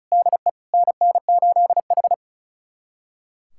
Morse code, heard as a keyed tone, about 34 wpm, 700 Hz. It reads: DE NN8H
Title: USA Callsign DE Callsign 35wpm